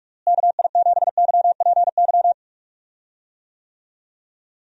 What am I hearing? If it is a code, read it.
Morse code, heard as a keyed tone, about 41 wpm, 700 Hz. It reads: KI7YPY
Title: USA Call Signs - 6 Characters - Repeated 2x - 45wpm